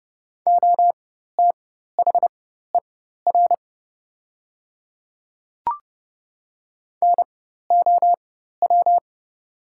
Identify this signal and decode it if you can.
Morse code, heard as a keyed tone, about 30 wpm, spaced as for 14 wpm, 700 Hz. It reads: OTHER NOW
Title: 4x Character Spacing - Top 200 Words - 30wpm